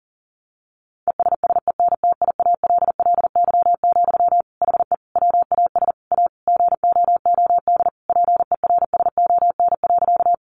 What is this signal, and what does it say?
Morse code, heard as a keyed tone, about 40 wpm, 700 Hz. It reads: ESSENTIALLY, HE WAS A GOOD PERSON.
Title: Sentences from Top 1300 Words 40wpm